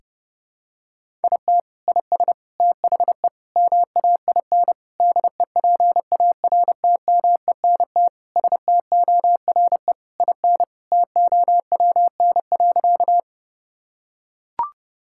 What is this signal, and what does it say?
Morse code, heard as a keyed tone, about 30 wpm, 700 Hz. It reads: IT IS THE MAIN DEPARTMENT STORE IN TOWN.
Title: Sentences from Top 1000 Words - Review Part 5 - 30wpm